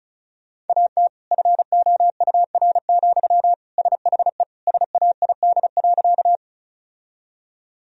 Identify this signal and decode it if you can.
Morse code, heard as a keyed tone, about 36 wpm, 700 Hz. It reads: AT FOUR, SHE SAID.
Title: Sentences from Top 1000 Words - Review Part 1 - 35wpm